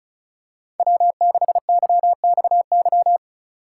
Morse code, heard as a keyed tone, about 34 wpm, 700 Hz. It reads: W6YXY
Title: USA Call Signs - 2x Character Spacing - 35wpm